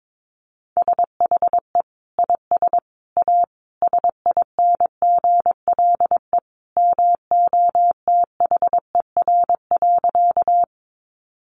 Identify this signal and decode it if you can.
Morse code, heard as a keyed tone, about 22 wpm, 700 Hz. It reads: SHE IS A SINGLE MOTHER.
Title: Sentences from Top 700 Words 22wpm